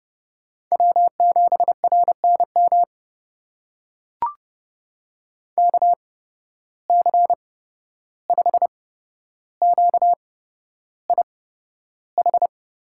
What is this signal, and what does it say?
Morse code, heard as a keyed tone, about 29 wpm, spaced as for 12 wpm, 700 Hz. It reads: W7RNM KC5QIH
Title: USA Call Signs - 8x Character Spacing - 30wpm